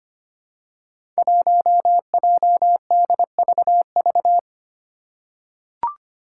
Morse code, heard as a keyed tone, about 25 wpm, 700 Hz. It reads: 1JDVV
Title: Five Challenging Letter-Number Combinations 25wpm